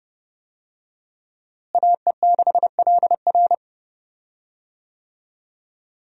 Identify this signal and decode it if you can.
Morse code, heard as a keyed tone, about 30 wpm, 700 Hz. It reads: AE6LR
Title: USA Call Signs - 5 Characters - Repeated 2x - 30wpm